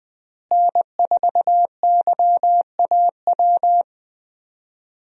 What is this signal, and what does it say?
Morse code, heard as a keyed tone, about 20 wpm, 700 Hz. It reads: N4YAW
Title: USA Call Signs - 5 Characters - Repeated 3x - 20wpm